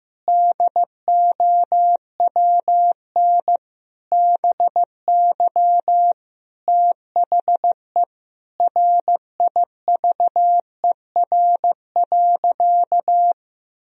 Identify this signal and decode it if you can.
Morse code, heard as a keyed tone, about 16 wpm, 700 Hz. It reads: DOWN BY THE RIVER.
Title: Sentences from Top 1000 Words - Review Part 4 - 15wpm